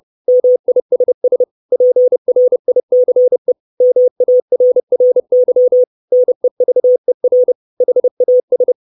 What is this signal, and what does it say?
Morse code, heard as a keyed tone, about 30 wpm, 500 Hz. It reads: MISS PRICE MARRY NEVER HAS
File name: 632 Words 30WPM